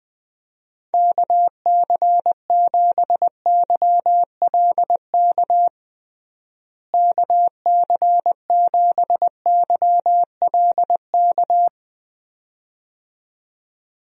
Morse code, heard as a keyed tone, about 20 wpm, 700 Hz. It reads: KC7YLK KC7YLK
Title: USA Call Signs - 6 Characters - Repeated 2x - 20wpm